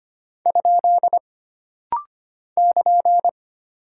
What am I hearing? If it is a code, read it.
Morse code, heard as a keyed tone, about 25 wpm, 700 Hz. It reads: ? <KN>
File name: Single Punctuation-Prosign - Rapid-Fire - 25wpm